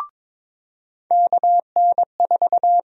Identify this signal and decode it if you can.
Morse code, heard as a keyed tone, about 22 wpm, 700 Hz. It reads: KN4
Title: USA Call Sign Prefixes - 22wpm